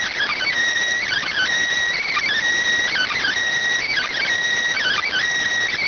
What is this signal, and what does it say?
Emergency Alert System (SAME) tones (header bursts), heard as harsh buzzing bursts. It is heard from 1.0 to 5.2 s.